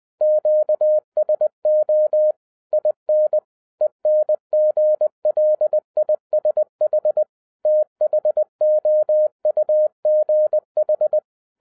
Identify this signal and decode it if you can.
Morse code, heard as a keyed tone, about 20 wpm, 610 Hz. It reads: QSO IN ENGLISH THOUGH